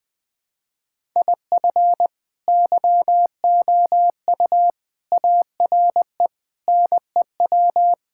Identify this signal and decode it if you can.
Morse code, heard as a keyed tone, about 21 wpm, 700 Hz. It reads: IF YOU ARE NEW
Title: Sentences with 8-Word Phrases - 20wpm